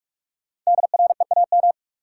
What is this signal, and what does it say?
Morse code, heard as a keyed tone, about 46 wpm, 700 Hz. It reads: DREAM
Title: Sentences from Top 1500 Words - New Word Drill - 45wpm